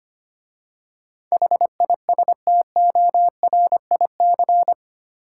Morse code, heard as a keyed tone, about 25 wpm, 700 Hz. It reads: HISTORIC
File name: 8 Letter Words - First 3 Letters Given - 25wpm